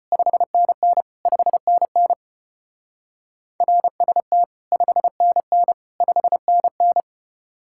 Morse code, heard as a keyed tone, about 32 wpm, 700 Hz. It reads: 5NN 5NN RST 5NN 5NN